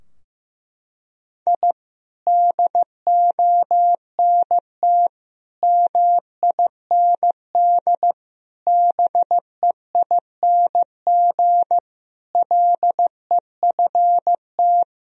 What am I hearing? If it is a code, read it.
Morse code, heard as a keyed tone, about 15 wpm, 700 Hz. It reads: I DONT MIND BEING LEFT
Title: Sentences from Top 1000 Words - Review Part 3 - 15wpm